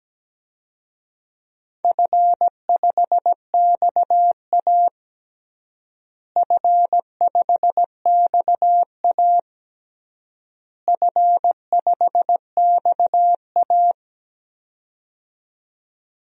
Morse code, heard as a keyed tone, about 17 wpm, 700 Hz. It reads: F5XA F5XA F5XA
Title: DXCC-100 Call Signs - 4 Characters - Repeated 3x - 17wpm